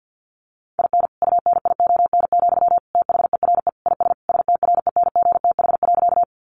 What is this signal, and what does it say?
Morse code, heard as a keyed tone, about 48 wpm, 700 Hz. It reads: IN UNION, THERE IS STRENGTH.
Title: Sentences from Top 1400 Words 50wpm